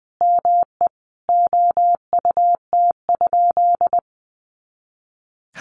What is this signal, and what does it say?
Morse code, heard as a keyed tone, about 20 wpm, 700 Hz. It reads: ME OUT?
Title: Sentences from Top 200 Words 20wpm